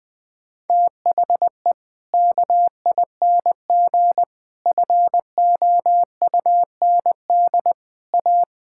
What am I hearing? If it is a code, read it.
Morse code, heard as a keyed tone, about 19 wpm, 700 Hz. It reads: THE KING FOUND A
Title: Sentences from Top 1000 Words - Review Part 5 - 20wpm